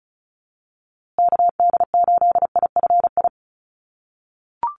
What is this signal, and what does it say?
Morse code, heard as a keyed tone, about 34 wpm, 700 Hz. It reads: KD8IFI
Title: USA Call Signs Repeated 2x 35wpm